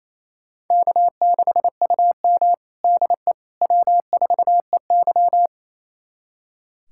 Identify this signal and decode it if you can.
Morse code, heard as a keyed tone, about 28 wpm, 700 Hz. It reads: K6UM DE W4EY
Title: USA Callsign DE Callsign - 4 Characters - 28wpm